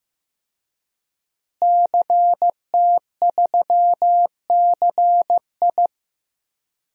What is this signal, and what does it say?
Morse code, heard as a keyed tone, about 15 wpm, 700 Hz. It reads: CT3CI
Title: DXCC-100 Call Signs - 5 Characters - Repeated 3x - 15wpm